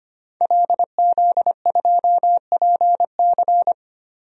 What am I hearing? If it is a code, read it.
Morse code, heard as a keyed tone, about 25 wpm, 700 Hz. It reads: LZ2PC
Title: DXCC-100 Call Signs - 5 Characters - 25wpm